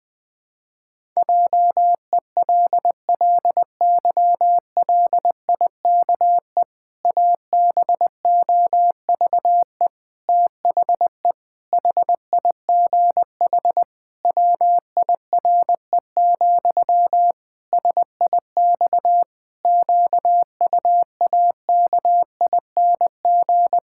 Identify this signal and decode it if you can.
Morse code, heard as a keyed tone, about 20 wpm, 700 Hz. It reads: JELLYLIKE ABOVE THE HIGH WIRE, SIX QUAKING
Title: Pangram Challenge v2 - 20wpm